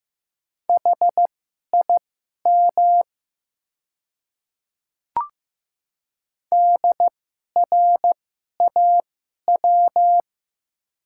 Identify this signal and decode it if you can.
Morse code, heard as a keyed tone, about 15 wpm, spaced as for 11 wpm, 700 Hz. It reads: HIM DRAW
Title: 2x Character Spacing - Top 500 Words - 15wpm